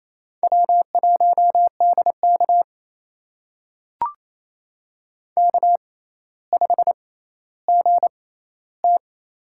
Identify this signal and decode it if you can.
Morse code, heard as a keyed tone, about 28 wpm, spaced as for 14 wpm, 700 Hz. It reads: W1DK K5GT
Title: USA Call Signs - 6x Character Spacing - 28wpm